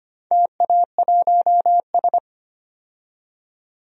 Morse code, heard as a keyed tone, about 25 wpm, 700 Hz. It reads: TA1S
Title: DXCC-100 Call Signs - 4 Characters - 25wpm